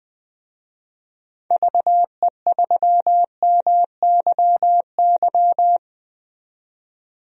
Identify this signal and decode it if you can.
Morse code, heard as a keyed tone, about 20 wpm, 700 Hz. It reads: VE3MYY
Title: DXCC-100 Call Signs - 6 Characters - 20wpm